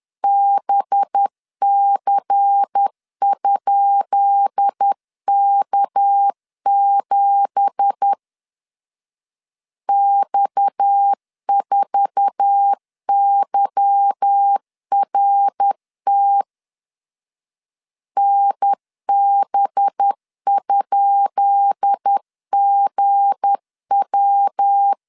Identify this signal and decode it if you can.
Morse code, heard as a keyed tone, about 10 wpm, 800 Hz. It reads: BC?K7 X4YRT NB?GW
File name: LEZIONI / 10 SIGNS / 10B INT 10wpm 2844